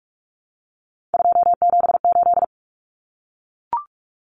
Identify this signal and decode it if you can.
Morse code, heard as a keyed tone, about 44 wpm, 700 Hz. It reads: 278
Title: Numbers - 3-Digit 45wpm